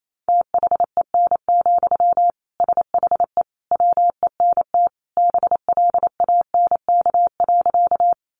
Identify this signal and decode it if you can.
Morse code, heard as a keyed tone, about 28 wpm, 700 Hz. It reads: THEN, SHE WENT BLANK.
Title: Sentences from Top 1500 Words 28wpm